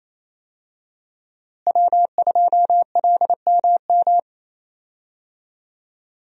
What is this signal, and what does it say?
Morse code, heard as a keyed tone, about 28 wpm, 700 Hz. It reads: W2LMM
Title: USA Call Signs - 5 Characters - Repeated 2x - 28wpm